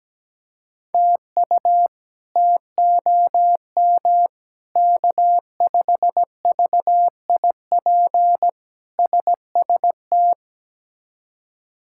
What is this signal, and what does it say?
Morse code, heard as a keyed tone, about 17 wpm, 700 Hz. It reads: TU TOM K5VIP SST
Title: K1USN SST QSO Practice - 17wpm